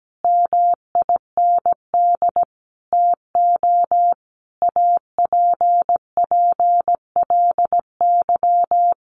Morse code, heard as a keyed tone, about 17 wpm, 700 Hz. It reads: MIND TO APPLY
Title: Sentences from Top 1400 Words 17wpm